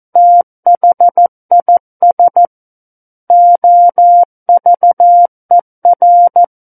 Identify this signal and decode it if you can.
Morse code, heard as a keyed tone, about 14 wpm, 700 Hz. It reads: THIS OVER
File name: Book One / 05-Chapter 5 - 14 WPM